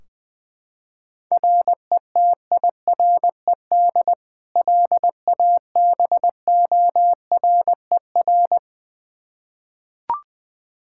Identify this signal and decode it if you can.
Morse code, heard as a keyed tone, about 20 wpm, 700 Hz. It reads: RETIRED LABORER
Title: Practice QSO - Exchange 3 - More Info - 20wpm